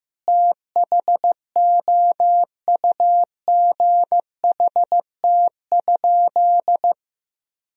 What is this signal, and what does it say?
Morse code, heard as a keyed tone, about 15 wpm, 700 Hz. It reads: THOUGHT?
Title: Sentences from Top 300 Words and Abbreviations 15wpm